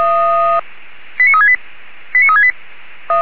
Five-tone selective call (ZVEI), heard as quick runs of five tones, tones of about 70 ms. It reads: 89368 89368